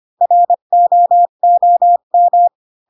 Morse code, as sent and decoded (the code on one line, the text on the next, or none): .-. --- --- --
ROOM